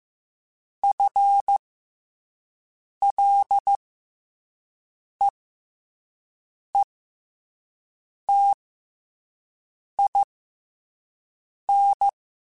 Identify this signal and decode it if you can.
Morse code, heard as a keyed tone, about 15 wpm, spaced as for 5 wpm, 800 Hz. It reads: FLEETIN
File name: FL / FL mp3s / FL Prefixes 3x2 Twice @5